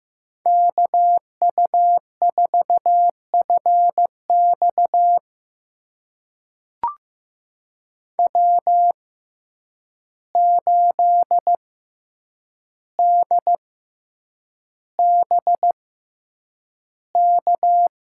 Morse code, heard as a keyed tone, about 15 wpm, spaced as for 8 wpm, 700 Hz. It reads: KU4FX W8DBK
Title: USA Call Signs - 5 Characters - 6x Character Spacing - 15wpm